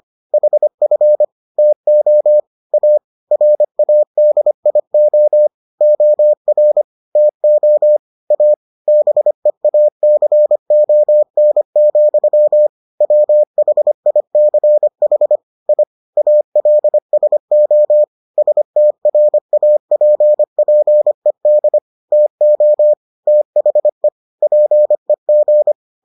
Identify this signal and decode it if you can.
Morse code, heard as a keyed tone, about 25 wpm, 600 Hz. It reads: HF TO A RADIO OR TO A BEACON, WHICH I ALSO STRAPPED TO THE PEG